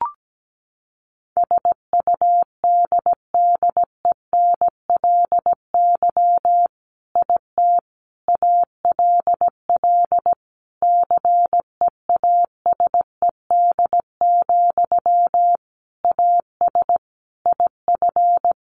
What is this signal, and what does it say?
Morse code, heard as a keyed tone, about 17 wpm, 700 Hz. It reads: SUDDENLY IT ALL CEASED, AS IF
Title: A Sound Of Thunder 17wpm